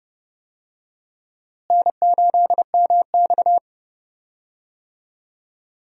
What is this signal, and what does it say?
Morse code, heard as a keyed tone, about 31 wpm, 700 Hz. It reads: N8MX
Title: USA Call Signs - 4 Characters - Repeated 2x - 30wpm